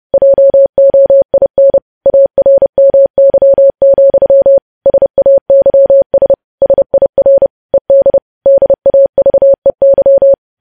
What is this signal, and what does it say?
Morse code, heard as a keyed tone, about 30 wpm, 550 Hz. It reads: JOIN ARMY, SAYS SIR ED DAVEY
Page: news headlines in morse code at 30 wpm